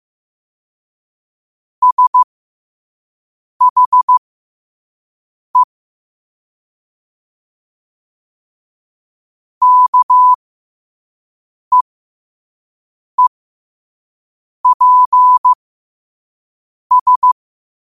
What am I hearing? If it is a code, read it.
Morse code, heard as a keyed tone, about 15 wpm, spaced as for 5 wpm, 1000 Hz. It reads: SHE KEEPS
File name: qotdfiles / 5wpm